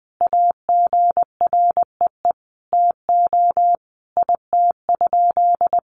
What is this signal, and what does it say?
Morse code, heard as a keyed tone, about 20 wpm, 700 Hz. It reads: AGREE TO IT?
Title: Sentences from Top 1100 Words 20wpm